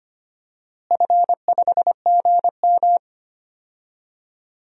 Morse code, heard as a keyed tone, about 25 wpm, 700 Hz. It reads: F5GM
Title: DXCC-100 Call Signs - 4 Characters - Repeated 3x - 25wpm